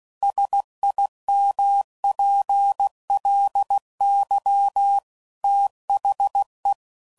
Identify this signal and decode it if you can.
Morse code, heard as a keyed tone, about 16 wpm, 800 Hz. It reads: SIMPLY THE
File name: CW Code Groups @15WPM/Technical Subjects